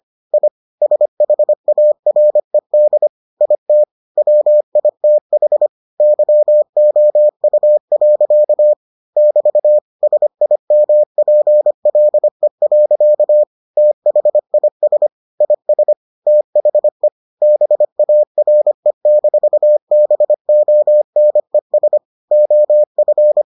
Morse code, heard as a keyed tone, about 25 wpm, 600 Hz. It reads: I SHARED IT WITH YOU. <BT> SIMPLE. THIS IS THE BARE-BONES OF